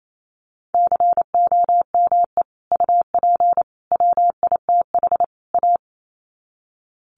Morse code, heard as a keyed tone, about 28 wpm, 700 Hz. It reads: COME UP WITH A
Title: Sets of 4 Words 28wpm